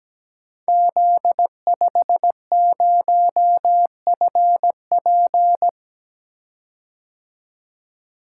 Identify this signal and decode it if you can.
Morse code, heard as a keyed tone, about 17 wpm, 700 Hz. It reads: Z50FP
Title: Code Groups - 5 Letter:Numbers - 17wpm